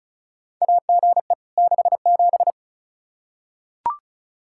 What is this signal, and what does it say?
Morse code, heard as a keyed tone, about 36 wpm, 700 Hz. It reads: AGE 67
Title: Practice QSO - Exchange 3 - More Info - 35wpm